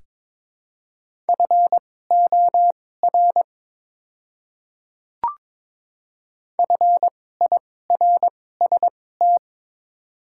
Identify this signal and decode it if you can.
Morse code, heard as a keyed tone, about 22 wpm, spaced as for 16 wpm, 700 Hz. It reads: FOR FIRST
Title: 2x Character Spacing - Top 100 Words - 22wpm